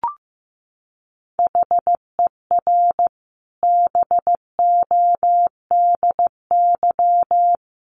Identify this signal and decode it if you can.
Morse code, heard as a keyed tone, about 15 wpm, 700 Hz. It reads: HER BODY
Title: Sentences from Top 1400 Words 15wpm